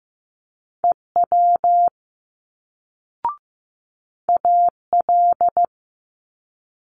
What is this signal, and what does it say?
Morse code, heard as a keyed tone, about 15 wpm, 700 Hz. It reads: EW AL
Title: Multi-letter Phonemes Quick Fire 15wpm